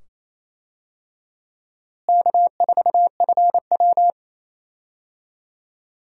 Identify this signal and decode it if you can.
Morse code, heard as a keyed tone, about 28 wpm, 700 Hz. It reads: K4FW
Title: USA Call Signs - 4 Characters - Repeated 2x - 28wpm